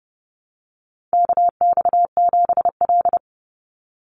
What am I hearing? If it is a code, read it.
Morse code, heard as a keyed tone, about 30 wpm, 700 Hz. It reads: KX7L
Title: NA SOTA Chaser Callsigns 30wpm